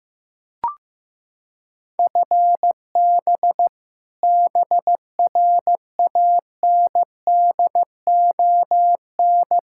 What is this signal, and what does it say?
Morse code, heard as a keyed tone, about 15 wpm, 700 Hz. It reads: FB BRANDON
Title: Practice QSO - Exchange 3 - More Info - 15wpm